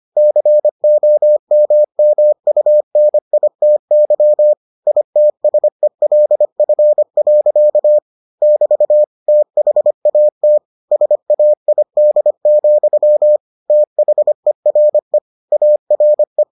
Morse code, heard as a keyed tone, about 25 wpm, 600 Hz. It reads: COMMUNITY ITSELF. <BT> THAT SAID, THERE ARE